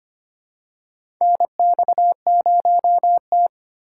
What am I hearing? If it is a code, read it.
Morse code, heard as a keyed tone, about 25 wpm, 700 Hz. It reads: NX0T
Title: USA Call Signs - 4 Characters - 3x Character Spacing - 25wpm